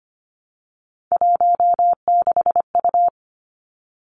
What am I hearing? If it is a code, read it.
Morse code, heard as a keyed tone, about 25 wpm, 700 Hz. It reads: 16U
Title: Three Challenging Letter-Number Combinations 25wpm